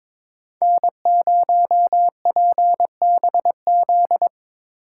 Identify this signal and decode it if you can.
Morse code, heard as a keyed tone, about 27 wpm, 700 Hz. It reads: N0PBZ
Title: Speed Racing - USA 1x3 Call Signs 22wpm